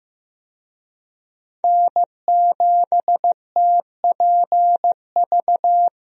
Morse code, heard as a keyed tone, about 16 wpm, 700 Hz. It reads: N7TPV
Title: USA Call Signs - 5 Characters - Repeated 2x - 15wpm